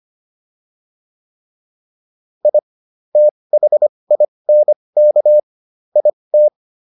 Morse code, heard as a keyed tone, about 25 wpm, spaced as for 20 wpm, 600 Hz. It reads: I THINK IT